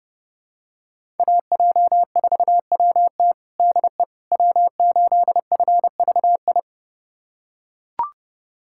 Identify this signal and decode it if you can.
Morse code, heard as a keyed tone, about 30 wpm, 700 Hz. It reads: AJ4WT DE W8FVI
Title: USA Callsign DE Callsign - 5 Characters - 30wpm